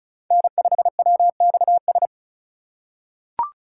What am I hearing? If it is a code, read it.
Morse code, heard as a keyed tone, about 36 wpm, 700 Hz. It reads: N5WXS
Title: USA Call Signs - 3x Character Spacing - 35wpm